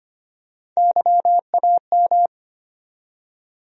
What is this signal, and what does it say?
Morse code, heard as a keyed tone, about 25 wpm, 700 Hz. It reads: YAM
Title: USA Call Sign Postfixes - 3 Letters 25wpm